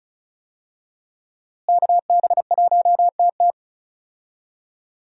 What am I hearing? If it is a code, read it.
Morse code, heard as a keyed tone, about 36 wpm, 700 Hz. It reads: KB1TT
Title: USA Call Signs - 5 Characters - Repeated 2x - 35wpm